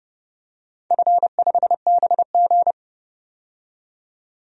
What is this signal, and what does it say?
Morse code, heard as a keyed tone, about 30 wpm, 700 Hz. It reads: F5BG
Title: DXCC-100 Call Signs - 4 Characters - 30wpm